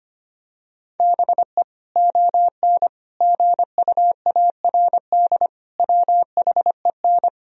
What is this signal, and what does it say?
Morse code, heard as a keyed tone, about 25 wpm, 700 Hz. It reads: BE ON GUARD WHEN
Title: Sentences from Top 1000 Words - Review Part 4 - 25wpm